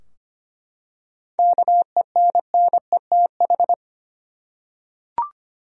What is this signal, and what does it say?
Morse code, heard as a keyed tone, about 25 wpm, 700 Hz. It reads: KENNETH
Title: USA Common Names - 25wpm